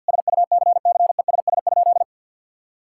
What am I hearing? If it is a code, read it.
Morse code, heard as a keyed tone, about 50 wpm, 700 Hz. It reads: SUCCESS?
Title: Sentences from Top 1000 Words - Review Part 6 - 50wpm